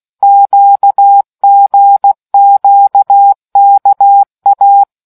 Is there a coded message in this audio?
Yes: Morse code from 0.2 to 4.8 s, heard as a keyed tone, about 16 wpm, 800 Hz. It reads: QGQKA